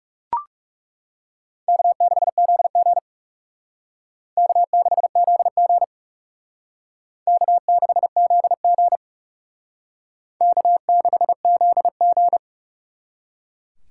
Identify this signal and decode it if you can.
Morse code, heard as a keyed tone, about 38 wpm, 700 Hz. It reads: K6ZG K6ZG K6ZG K6ZG